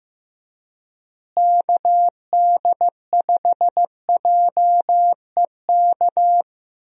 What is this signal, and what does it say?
Morse code, heard as a keyed tone, about 16 wpm, 700 Hz. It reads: KD5JEK
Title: USA Call Signs - 6 Characters - Repeated 3x - 15wpm